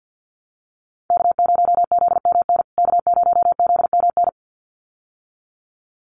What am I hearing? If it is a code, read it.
Morse code, heard as a keyed tone, about 50 wpm, 700 Hz. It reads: K0ZMN K0ZMN
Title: USA Call Signs Repeated 2x 50wpm